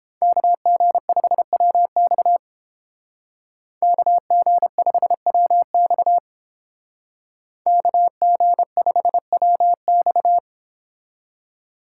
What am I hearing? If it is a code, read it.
Morse code, heard as a keyed tone, about 30 wpm, 700 Hz. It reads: KG5WX KG5WX KG5WX